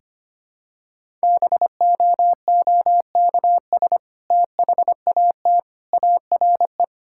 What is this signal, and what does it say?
Morse code, heard as a keyed tone, about 25 wpm, 700 Hz. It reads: BOOKS THAT ARE
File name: practice / 25wpm / Sentences with 2-Word Phrases - 25wpm